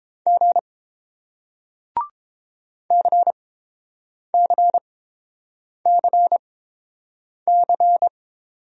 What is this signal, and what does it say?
Morse code, heard as a keyed tone, about 31 wpm, spaced as for 7 wpm, 700 Hz. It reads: G CCCC